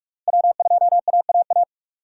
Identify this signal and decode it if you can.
Morse code, heard as a keyed tone, about 48 wpm, 690 Hz. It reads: W2AAA
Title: USA Call Signs - 5 Characters - 2x Character Spacing - 45wpm